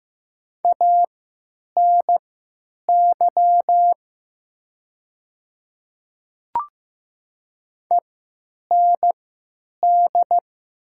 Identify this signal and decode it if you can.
Morse code, heard as a keyed tone, about 15 wpm, spaced as for 9 wpm, 700 Hz. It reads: ANY END
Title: 3x Character Spacing - Top 200 Words - 15wpm